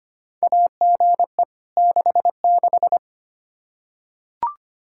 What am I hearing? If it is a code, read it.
Morse code, heard as a keyed tone, about 25 wpm, 700 Hz. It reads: AGE 66
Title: Practice QSO - Exchange 3 - More Info - 25wpm